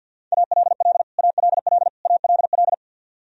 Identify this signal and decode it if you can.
Morse code, heard as a keyed tone, about 51 wpm, 700 Hz. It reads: ALL ALL ALL
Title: Top 100 Words Repeated 50wpm